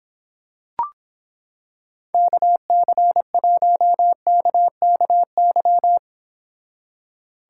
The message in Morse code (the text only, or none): KC1KKY